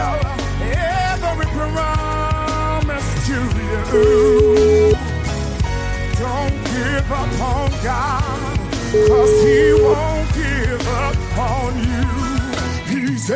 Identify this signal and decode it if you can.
Ringback tone (Europe, 425 Hz), with noise.